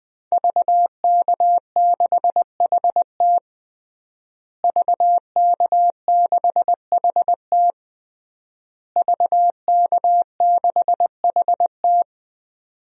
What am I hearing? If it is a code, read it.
Morse code, heard as a keyed tone, about 20 wpm, 700 Hz. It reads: VK6HT VK6HT VK6HT
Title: DXCC-100 Call Signs - 5 Characters - Repeated 3x - 20wpm